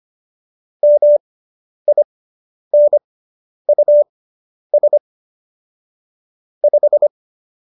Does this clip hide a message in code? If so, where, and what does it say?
Yes: Morse code from 0.8 to 7.1 s, heard as a keyed tone, about 25 wpm, spaced as for 10 wpm, 590 Hz. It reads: MINUS 5